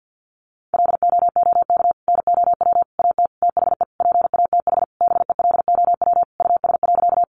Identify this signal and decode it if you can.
Morse code, heard as a keyed tone, about 50 wpm, 700 Hz. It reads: LOOK NOW AT THE PATH BELOW US.
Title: Sentences from Top 1400 Words 50wpm